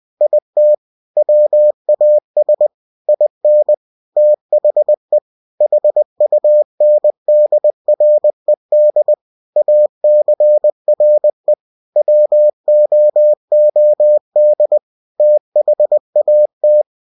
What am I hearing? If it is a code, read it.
Morse code, heard as a keyed tone, about 20 wpm, 600 Hz. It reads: IT WAS IN THE HUNDRED ACRE WOOD THAT
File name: cw-books / winnie-the-pooh / winnie-20wpm-page-0028